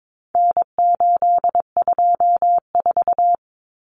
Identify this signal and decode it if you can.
Morse code, heard as a keyed tone, about 22 wpm, 700 Hz. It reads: N824
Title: Common Cut Numbers - 4-Digit 22wpm